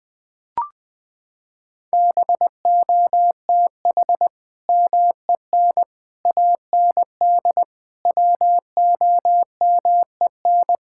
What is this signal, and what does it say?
Morse code, heard as a keyed tone, about 20 wpm, 700 Hz. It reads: BOTH MEN AND WOMEN
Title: Sentences with 3-Word Phrases - 20wpm